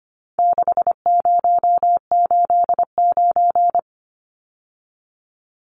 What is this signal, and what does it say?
Morse code, heard as a keyed tone, about 25 wpm, 700 Hz. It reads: 6089
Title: Numbers - 4-Digit 25wpm